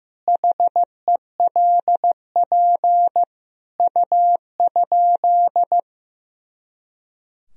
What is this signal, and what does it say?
Morse code, heard as a keyed tone, about 15 wpm, 700 Hz. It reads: HELP U?
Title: Sentences from Top 200 Words and Abbreviations 15wpm